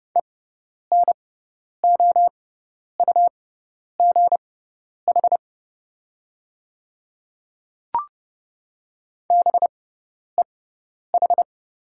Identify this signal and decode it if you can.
Morse code, heard as a keyed tone, about 30 wpm, spaced as for 10 wpm, 690 Hz. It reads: ENOUGH BEH
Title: 6x Character Spacing - Top 500 Words - 30wpm